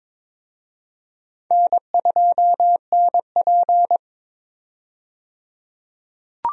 Morse code, heard as a keyed tone, about 22 wpm, 700 Hz. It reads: N2NP
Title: USA Call Signs - 4 Characters - 22wpm